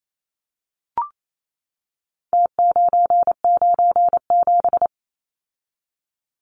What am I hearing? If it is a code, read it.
Morse code, heard as a keyed tone, about 28 wpm, 700 Hz. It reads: T997
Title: Numbers - Contest Style - 4-Digit 28wpm